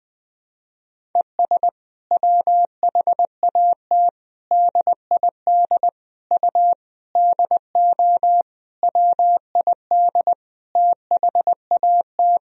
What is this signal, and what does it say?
Morse code, heard as a keyed tone, about 20 wpm, 700 Hz. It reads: ES WHAT DID U DO WID THAT
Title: Sentences from Top 300 Words and Abbreviations 20wpm